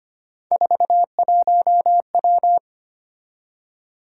Morse code, heard as a keyed tone, about 25 wpm, 700 Hz. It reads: 41W
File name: Three Letter-Number Combinations - 25wpm